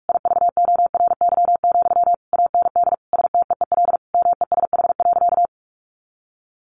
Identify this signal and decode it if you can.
Morse code, heard as a keyed tone, about 46 wpm, 700 Hz. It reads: IVORY, AND STEEL MESH.
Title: A Sound Of Thunder 45wpm